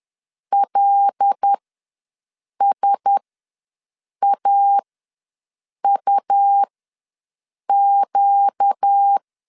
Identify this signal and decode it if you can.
Morse code, heard as a keyed tone, about 10 wpm, spaced as for 6 wpm, 800 Hz. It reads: LSAUQ
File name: LEZIONI / 5 LBVC / 5 LB ws 10wpm 3616